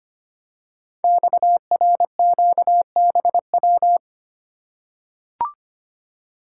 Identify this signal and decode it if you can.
Morse code, heard as a keyed tone, about 25 wpm, 700 Hz. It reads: XRQBW
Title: Code Groups - 5 Letters - 25wpm